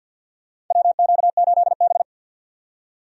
Morse code, heard as a keyed tone, about 48 wpm, 700 Hz. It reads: WQ8B
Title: USA Call Signs - 4 Characters - 50wpm